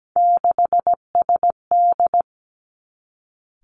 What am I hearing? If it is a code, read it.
Morse code, heard as a keyed tone, about 17 wpm, 700 Hz. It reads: 6SD
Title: Three Challenging Letter-Number Combinations 17wpm